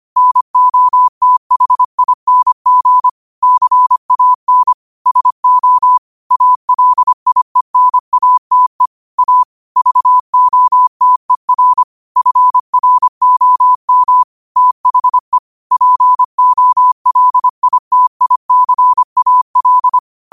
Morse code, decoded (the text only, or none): NOTHING CAN SO ALIENATE A VOTER FROM THE POLITICAL